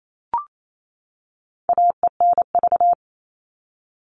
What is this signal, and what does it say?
Morse code, heard as a keyed tone, about 28 wpm, 700 Hz. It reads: AENV
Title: All Cut Numbers - 4-Digit 28wpm